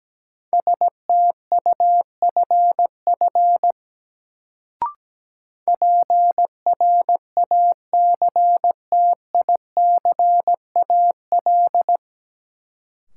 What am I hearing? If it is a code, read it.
Morse code, heard as a keyed tone, about 17 wpm, 700 Hz. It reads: STUFF PRACTICAL
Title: Sentences from Top 1400 Words - New Word Drill - 17wpm